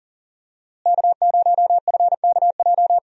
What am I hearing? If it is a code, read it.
Morse code, heard as a keyed tone, about 40 wpm, 700 Hz. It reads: K0FKJ
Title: USA Call Signs - 5 Characters - Repeated 2x - 40wpm